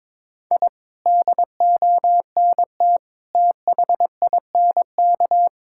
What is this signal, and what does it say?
Morse code, heard as a keyed tone, about 22 wpm, 700 Hz. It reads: I DONT THINK
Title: Sentences from Top 1000 Words - Review Part 6 - 22wpm